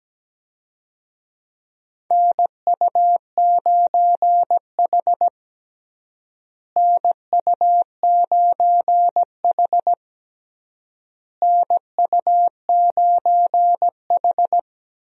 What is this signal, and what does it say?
Morse code, heard as a keyed tone, about 17 wpm, 700 Hz. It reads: NU9H NU9H NU9H
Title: USA Call Signs - 4 Characters - Repeated 3x - 17wpm